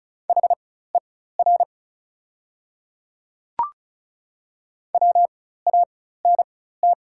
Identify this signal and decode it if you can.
Morse code, heard as a keyed tone, about 36 wpm, spaced as for 16 wpm, 690 Hz. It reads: HER WANT
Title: 4x Character Spacing - Top 200 Words - 35wpm